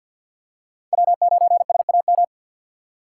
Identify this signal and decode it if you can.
Morse code, heard as a keyed tone, about 47 wpm, 700 Hz. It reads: W9SAM
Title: USA Call Signs - 5 Characters - 6x Character Spacing - 50wpm